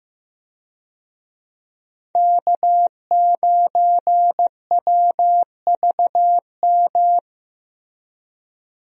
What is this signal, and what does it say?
Morse code, heard as a keyed tone, about 16 wpm, 700 Hz. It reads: K9WVM
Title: USA Call Signs - 5 Characters - Repeated 3x - 15wpm